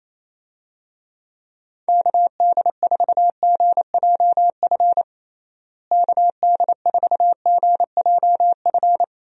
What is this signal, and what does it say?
Morse code, heard as a keyed tone, about 28 wpm, 700 Hz. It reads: KD4GJF KD4GJF
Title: USA Call Signs - 6 Characters - Repeated 3x - 28wpm